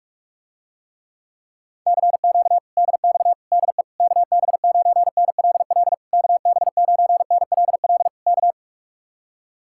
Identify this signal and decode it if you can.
Morse code, heard as a keyed tone, about 49 wpm, 700 Hz. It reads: CQ DX DE KB9NLL KB9NLL K